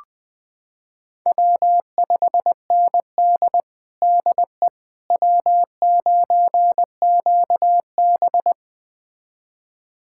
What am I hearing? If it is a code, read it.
Morse code, heard as a keyed tone, about 20 wpm, 700 Hz. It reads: W5ND DE W9QB
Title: USA Callsign DE Callsign - 4 Characters - 20wpm